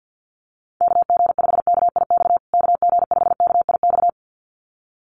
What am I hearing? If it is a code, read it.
Morse code, heard as a keyed tone, about 48 wpm, 700 Hz. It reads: KG5KIX KG5KIX
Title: USA Call Signs Repeated 2x 50wpm